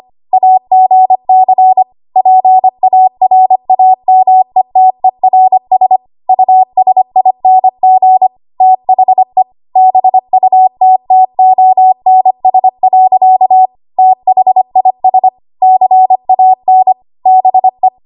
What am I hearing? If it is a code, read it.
Morse code, heard as a keyed tone, about 25 wpm, 750 Hz. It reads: AGC PARAMETERS USING THE BUTTONS. THIS CAN BE
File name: Archive / 25 WPM / 25WPM